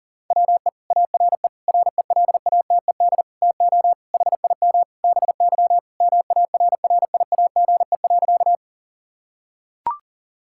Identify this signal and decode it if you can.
Morse code, heard as a keyed tone, about 40 wpm, 700 Hz. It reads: WE ARE RELATED TO HIM BY MARRIAGE.
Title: Sentences from Top 1000 Words - Review Part 5 - 40wpm